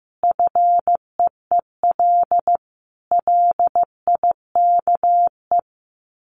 Morse code, heard as a keyed tone, about 15 wpm, 700 Hz. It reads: FEEL LIKE
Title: Sets of 5 Words Encore 15wpm